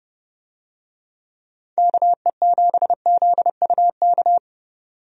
Morse code, heard as a keyed tone, about 30 wpm, 700 Hz. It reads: KE7ZUK
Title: USA Call Signs - 6 Characters - 30wpm